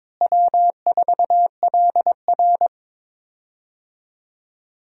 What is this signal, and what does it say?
Morse code, heard as a keyed tone, about 22 wpm, 700 Hz. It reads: W4LR
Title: USA Call Signs - 4 Characters - Repeated 2x - 22wpm